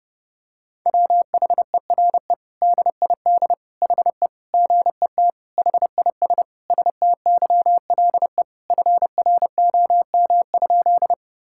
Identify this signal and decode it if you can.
Morse code, heard as a keyed tone, about 32 wpm, 700 Hz. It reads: WHERE DID HE GET HIS STYLE FROM?
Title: Sentences from Top 1000 Words - Review Part 5 - 30wpm